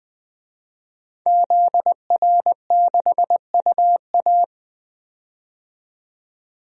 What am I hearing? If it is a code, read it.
Morse code, heard as a keyed tone, about 20 wpm, 700 Hz. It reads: ZR6UA
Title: DXCC-100 Call Signs - 5 Characters - 20wpm